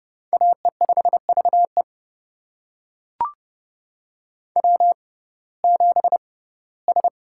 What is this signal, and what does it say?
Morse code, heard as a keyed tone, about 32 wpm, spaced as for 19 wpm, 700 Hz. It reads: AE5VE W7S
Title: USA Call Signs - 5 Characters - 6x Character Spacing - 30wpm